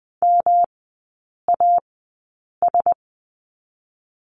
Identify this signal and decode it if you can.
Morse code, heard as a keyed tone, about 20 wpm, spaced as for 8 wpm, 700 Hz. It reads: MAS
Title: 159 Lesson 26e - Letter g - Three Characters 20wpm